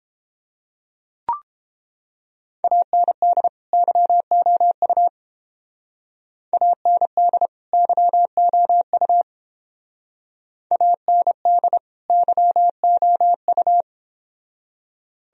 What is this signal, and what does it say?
Morse code, heard as a keyed tone, about 29 wpm, 700 Hz. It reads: AND YOU AND YOU AND YOU